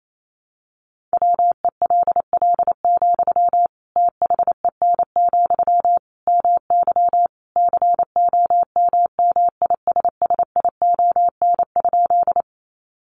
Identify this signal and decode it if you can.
Morse code, heard as a keyed tone, about 28 wpm, 700 Hz. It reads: WELL, THEN, MY COMMISSION?
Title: Sentences from Top 1400 Words 28wpm